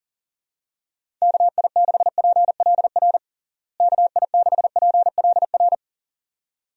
Morse code, heard as a keyed tone, about 40 wpm, 700 Hz. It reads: KI6PLR KI6PLR
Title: USA Call Signs - 6 Characters - Repeated 2x - 40wpm